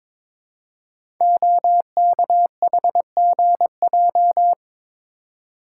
Morse code, heard as a keyed tone, about 22 wpm, 700 Hz. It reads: OKHGJ
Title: Code Groups - 5 Letter:Numbers - 22wpm